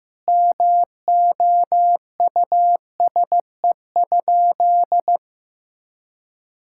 Morse code, heard as a keyed tone, about 16 wpm, 700 Hz. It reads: MOUSE?
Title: Sentences from Top 1000 Words - Review Part 5 - 15wpm